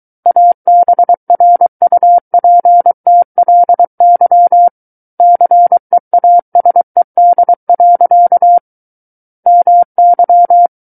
Morse code, heard as a keyed tone, about 23 wpm, 700 Hz. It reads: ABRUPTLY CEASED. MY
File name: Book One / 14-Chapter 14 - 23 WPM